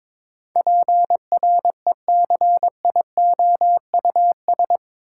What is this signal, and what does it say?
Morse code, heard as a keyed tone, about 22 wpm, 700 Hz. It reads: PRECIOUS
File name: practice / 22wpm / Eight Letter Words - 22wpm